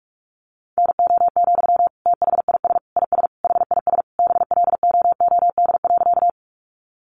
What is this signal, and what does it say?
Morse code, heard as a keyed tone, about 44 wpm, 700 Hz. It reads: NO, THIS IS HIS BLOOD.
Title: Sentences from Top 1100 Words 45wpm